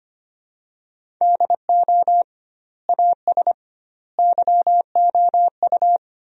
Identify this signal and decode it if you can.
Morse code, heard as a keyed tone, about 25 wpm, 700 Hz. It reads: DO AS YOU
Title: 2x Word-Spacing - Sentences from Top 100 Words - 25wpm